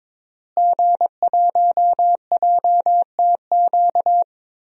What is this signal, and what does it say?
Morse code, heard as a keyed tone, about 22 wpm, 700 Hz. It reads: G1JTQ
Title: DXCC-100 Call Signs - 5 Characters - Repeated 3x - 22wpm